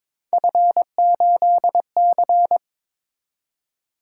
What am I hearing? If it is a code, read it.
Morse code, heard as a keyed tone, about 22 wpm, 700 Hz. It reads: F8C
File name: practice / 22wpm / Three Letter-Number Combinations - 22wpm